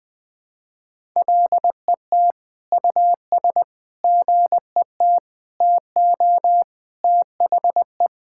Morse code, heard as a keyed tone, about 20 wpm, 700 Hz. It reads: LET US GET TO THE
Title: Sentences from Top 1600 Words 20wpm